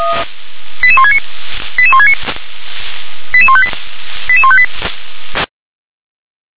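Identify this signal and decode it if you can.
Five-tone selective call (ZVEI), heard as quick runs of five tones, tones of about 80 ms. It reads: 88169 88169 88158 88158
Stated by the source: ZVEI five-tone call received on SDR.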